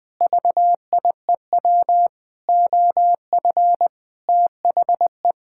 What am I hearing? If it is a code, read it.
Morse code, heard as a keyed tone, about 20 wpm, 700 Hz. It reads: VIEW OF THE
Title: Sentences with 7-Word Phrases - Spoken Beginning - 20wpm